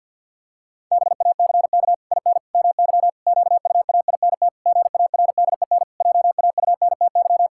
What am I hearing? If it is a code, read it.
Morse code, heard as a keyed tone, about 50 wpm, 700 Hz. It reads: BACK IN MY QUAINT GARDEN JAUNTY
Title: Pangram Challenge v2 - 50wpm